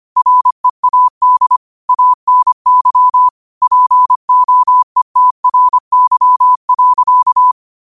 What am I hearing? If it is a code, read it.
Morse code, heard as a keyed tone, about 25 wpm, 1000 Hz. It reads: READ ANY POETRY.
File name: qotdfiles / 25wpm